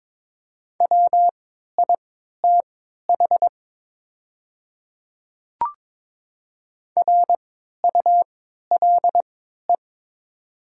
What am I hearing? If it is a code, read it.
Morse code, heard as a keyed tone, about 22 wpm, spaced as for 12 wpm, 700 Hz. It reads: WITH RULE
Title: 3x Character Spacing - Top 500 Words - 22wpm